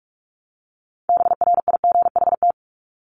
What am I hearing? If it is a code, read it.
Morse code, heard as a keyed tone, about 44 wpm, 700 Hz. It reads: BRIGHT
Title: Top 751-1000 Words 45wpm